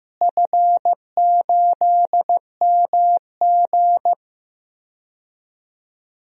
Morse code, heard as a keyed tone, about 15 wpm, 700 Hz. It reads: F8MG
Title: DXCC-100 Call Signs - 4 Characters - Repeated 2x - 15wpm